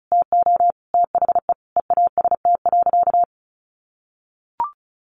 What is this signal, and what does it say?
Morse code, heard as a keyed tone, about 34 wpm, 700 Hz. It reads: TO THE EAST.
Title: Sentences from Top 1300 Words 35wpm